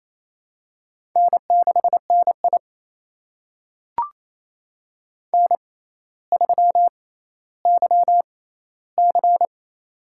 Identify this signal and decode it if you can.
Morse code, heard as a keyed tone, about 28 wpm, spaced as for 14 wpm, 700 Hz. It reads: N6NI N3YC
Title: USA Call Signs - 4 Characters - 6x Character Spacing - 28wpm